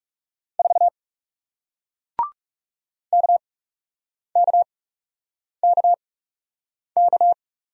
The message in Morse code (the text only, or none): 4 KKKK